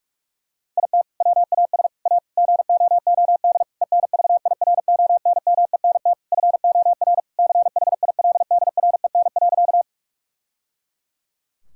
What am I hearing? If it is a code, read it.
Morse code, heard as a keyed tone, about 41 wpm, 700 Hz. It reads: IT WAS A GOOD ENVIRONMENT FOR CHILDREN.
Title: Sentences from Top 1000 Words - Review Part 5 - 45wpm